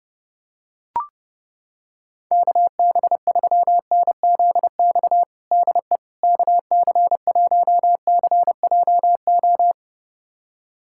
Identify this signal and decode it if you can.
Morse code, heard as a keyed tone, about 30 wpm, 700 Hz. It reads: KB3NZX DE KC1CJO
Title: USA Callsign DE Callsign 30wpm